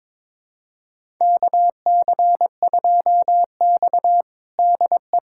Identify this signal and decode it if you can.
Morse code, heard as a keyed tone, about 22 wpm, 700 Hz. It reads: KC2X DE
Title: USA Callsign DE Callsign - 4 Characters - 22wpm